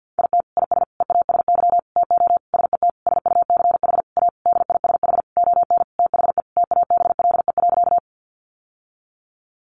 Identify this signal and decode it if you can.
Morse code, heard as a keyed tone, about 50 wpm, 700 Hz. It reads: IT IS EASY TO SET SUCH A DISH ON THE TABLE.
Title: Sentences from Top 1300 Words 50wpm